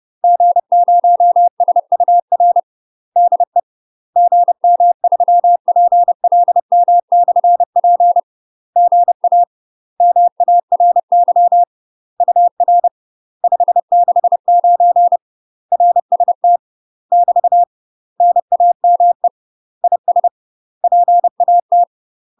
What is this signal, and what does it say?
Morse code, heard as a keyed tone, about 30 wpm, 700 Hz. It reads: G0SUR DE GM3PLM/P GA MARY UR 569 RST <BT> NAME IS PAT
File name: UK MASTER QSOs TRAINING / 30WPM / UK MASTER QSOs 0034